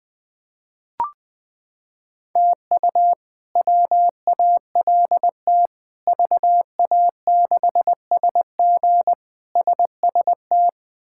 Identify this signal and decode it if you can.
Morse code, heard as a keyed tone, about 21 wpm, 700 Hz. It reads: TU WALT VA6SG SST
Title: K1USN SST QSO Practice - 20wpm